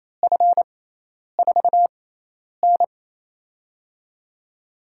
Morse code, heard as a keyed tone, about 28 wpm, spaced as for 10 wpm, 700 Hz. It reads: F4N
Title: USA Call Signs - 4 Characters - 6x Character Spacing - 28wpm